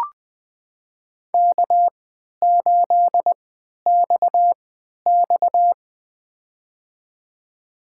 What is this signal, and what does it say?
Morse code, heard as a keyed tone, about 20 wpm, spaced as for 11 wpm, 700 Hz. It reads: K8XX
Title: USA Call Signs - 4 Characters - 3x Character Spacing - 20wpm